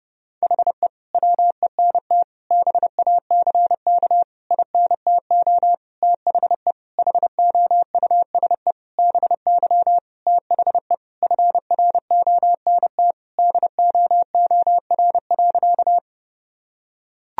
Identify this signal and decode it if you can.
Morse code, heard as a keyed tone, about 30 wpm, 700 Hz. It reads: HE WENT BACK INTO THE HOUSE BY THE FRONT DOOR.
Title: Sentences from Top 1000 Words - Review Part 2 - 30wpm